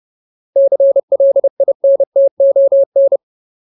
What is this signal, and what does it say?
Morse code, heard as a keyed tone, about 30 wpm, 550 Hz. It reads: CLINTON
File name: Ultimate Word List / 30WPM Ultimate